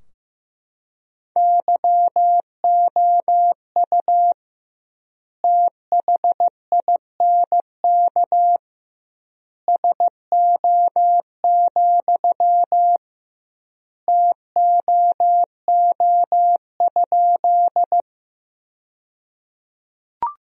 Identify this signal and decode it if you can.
Morse code, heard as a keyed tone, about 15 wpm, 700 Hz. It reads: YOU THINK SO, TOO?
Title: 2x Word-Spacing - Sentences from Top 200 Words - 15wpm